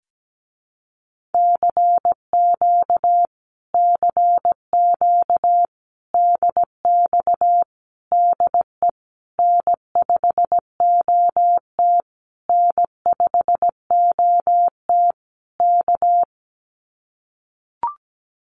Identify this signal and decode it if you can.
Morse code, heard as a keyed tone, about 17 wpm, 700 Hz. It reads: CQ CQ DX DE N5OT N5OT K
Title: Common QSO Phrases Encore 17wpm